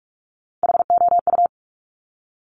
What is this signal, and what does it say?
Morse code, heard as a keyed tone, about 46 wpm, 700 Hz. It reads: HOU
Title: Top Three Letter Combinations in Words 45wpm